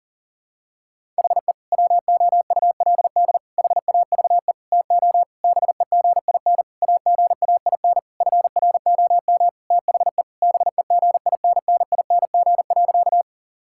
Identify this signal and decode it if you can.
Morse code, heard as a keyed tone, about 40 wpm, 700 Hz. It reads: HE WOULD HAVE TO BEGIN AGAIN FROM THE BEGINNING.
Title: Sentences from Top 1000 Words - Review Part 5 - 40wpm